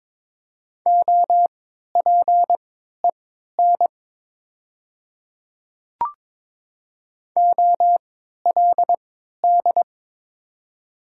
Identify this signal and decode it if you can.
Morse code, heard as a keyed tone, about 21 wpm, spaced as for 12 wpm, 700 Hz. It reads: OPEN OLD
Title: 3x Character Spacing - Top 500 Words - 22wpm